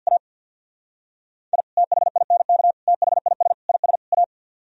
Morse code, heard as a keyed tone, about 65 wpm, spaced as for 57 wpm, 700 Hz. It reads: A I THINK THIS IS A
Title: Speed-Racing - Sets of 5 Words 50wpm